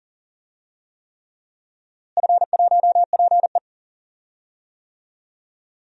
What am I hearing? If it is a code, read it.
Morse code, heard as a keyed tone, about 40 wpm, 700 Hz. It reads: F1PE
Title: DXCC-100 Call Signs - 4 Characters - Repeated 3x - 40wpm